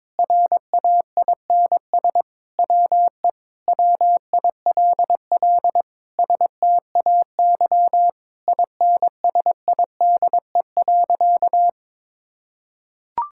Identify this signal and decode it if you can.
Morse code, heard as a keyed tone, about 21 wpm, 700 Hz. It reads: RAINS WE WILL STAY INSIDE.
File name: practice / 22wpm / Sentences with 2-Word Phrases - 22wpm